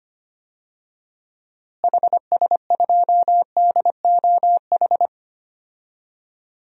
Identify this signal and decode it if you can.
Morse code, heard as a keyed tone, about 25 wpm, 700 Hz. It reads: HS2DOH
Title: DXCC-100 Call Signs - 6 Characters - Repeated 2x - 25wpm